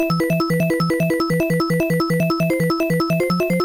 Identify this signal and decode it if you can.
There is no code.